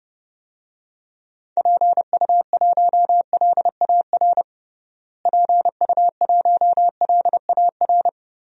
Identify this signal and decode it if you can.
Morse code, heard as a keyed tone, about 30 wpm, 700 Hz. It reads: PU1LAR PU1LAR
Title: DXCC-100 Call Signs - 6 Characters - Repeated 3x - 30wpm